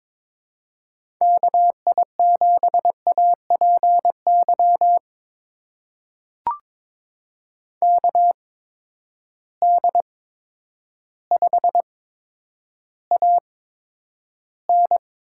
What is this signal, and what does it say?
Morse code, heard as a keyed tone, about 22 wpm, spaced as for 10 wpm, 700 Hz. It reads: KI7APY KD5AN
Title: USA Call Signs - 6 Characters - 8x Character Spacing - 22wpm